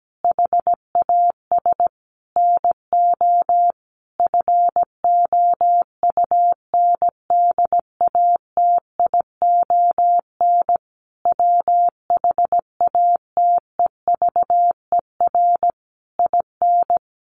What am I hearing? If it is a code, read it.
Morse code, heard as a keyed tone, about 17 wpm, 700 Hz. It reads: HAS NO FOUNDATION WHATEVER IN
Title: Sentences from Top 1200 Words 17wpm